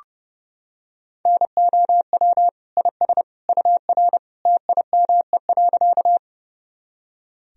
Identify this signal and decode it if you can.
Morse code, heard as a keyed tone, about 30 wpm, 700 Hz. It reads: NOW IS UR TIME.
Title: Sentences from Top 100 Words and Abbreviations 30wpm